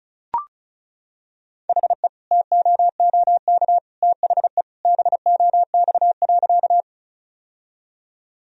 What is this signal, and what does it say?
Morse code, heard as a keyed tone, about 36 wpm, 700 Hz. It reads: HE TOOK THE BOX.
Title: Sentences from Top 1000 Words - Review Part 2 - 35wpm